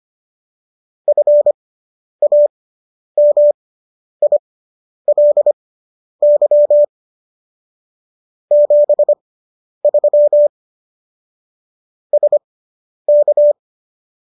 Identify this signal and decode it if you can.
Morse code, heard as a keyed tone, about 25 wpm, spaced as for 10 wpm, 590 Hz. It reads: FAMILY 73 SK
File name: qso219 10